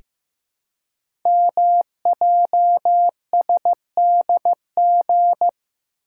Morse code, heard as a keyed tone, about 15 wpm, 700 Hz. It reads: MJSDG
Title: Code Groups - 5 Letters - 15wpm